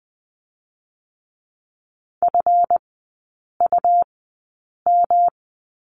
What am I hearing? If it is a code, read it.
Morse code, heard as a keyed tone, about 20 wpm, spaced as for 8 wpm, 700 Hz. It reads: FUM
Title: 205 Lesson 33e - Letter j - Three Characters 20wpm